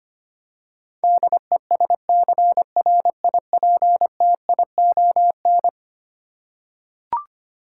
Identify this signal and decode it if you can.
Morse code, heard as a keyed tone, about 25 wpm, 700 Hz. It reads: DESCRIPTION
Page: morse-code-ninja-practice-25wpm